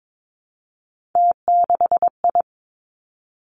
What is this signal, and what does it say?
Morse code, heard as a keyed tone, about 22 wpm, 700 Hz. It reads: T6I
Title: Three Challenging Letter-Number Combinations 22wpm